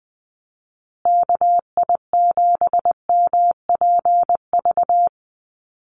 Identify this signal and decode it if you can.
Morse code, heard as a keyed tone, about 20 wpm, 700 Hz. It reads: KI7MPV
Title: USA Call Signs Repeated 3x 20wpm